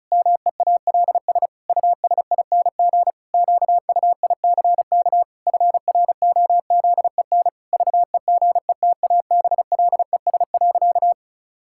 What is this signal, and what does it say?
Morse code, heard as a keyed tone, about 36 wpm, 700 Hz. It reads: MEALS USING QUICK FROZEN VEGETABLES.
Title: Pangram Challenge v2 - 35wpm